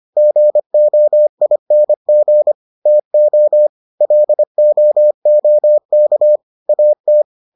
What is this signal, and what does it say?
Morse code, heard as a keyed tone, about 25 wpm, 600 Hz. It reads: GOING TO LOOK AT